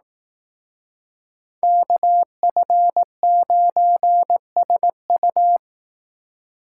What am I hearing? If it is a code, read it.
Morse code, heard as a keyed tone, about 18 wpm, 700 Hz. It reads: KF9SU
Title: Speed Racing - USA 2x2 Call Signs 15wpm